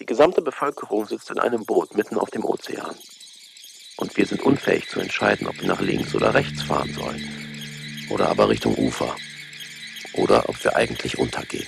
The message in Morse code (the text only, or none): none